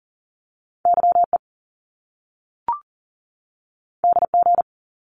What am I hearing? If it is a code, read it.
Morse code, heard as a keyed tone, about 39 wpm, 700 Hz. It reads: YE DG
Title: Top Two Letter Combinations in Words 40wpm